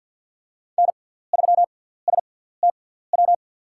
Morse code, heard as a keyed tone, about 50 wpm, spaced as for 17 wpm, 700 Hz. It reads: N3STW
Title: USA Call Signs - 6x Character Spacing - 50wpm